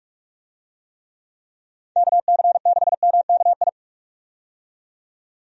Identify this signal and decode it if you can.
Morse code, heard as a keyed tone, about 44 wpm, 700 Hz. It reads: KC6MKI
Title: USA Call Signs - 6 Characters - 45wpm